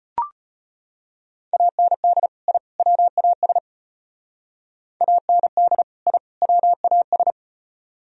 Morse code, heard as a keyed tone, about 33 wpm, 700 Hz. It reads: AND I WAS AND I WAS